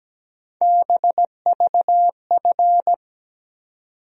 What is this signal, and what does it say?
Morse code, heard as a keyed tone, about 17 wpm, 700 Hz. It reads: BVF
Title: USA Call Sign Postfixes - 3 Letters 17wpm